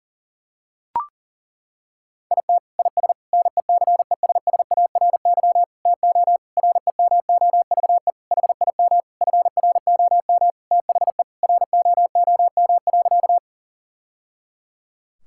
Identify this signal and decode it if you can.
Morse code, heard as a keyed tone, about 40 wpm, 700 Hz. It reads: IT IS NECESSARY TO REMOVE HIM FROM THE ROOM.
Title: Sentences from Top 1000 Words - Review Part 5 - 40wpm